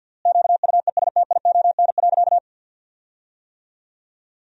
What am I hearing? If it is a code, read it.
Morse code, heard as a keyed tone, about 49 wpm, 700 Hz. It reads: QUESTION.
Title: 3x Word-Spacing - Sentences from Top 400 Words - 50wpm